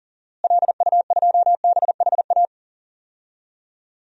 Morse code, heard as a keyed tone, about 40 wpm, 700 Hz. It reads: LU2BHA
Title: DXCC-100 Call Signs - 6 Characters - Repeated 2x - 40wpm